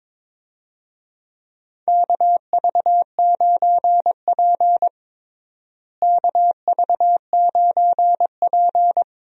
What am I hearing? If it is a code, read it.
Morse code, heard as a keyed tone, about 22 wpm, 700 Hz. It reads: KV9P KV9P
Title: USA Call Signs - 4 Characters - Repeated 3x - 22wpm